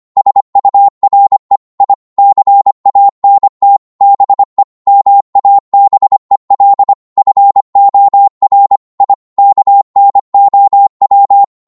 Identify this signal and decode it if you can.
Morse code, heard as a keyed tone, about 25 wpm, 800 Hz. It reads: SURE I CANT BE MABEL FOR I KNOW